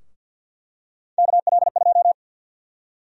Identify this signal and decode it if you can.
Morse code, heard as a keyed tone, about 48 wpm, 700 Hz. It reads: KL2
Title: USA Call Sign Prefixes - Repeated 2x - 50wpm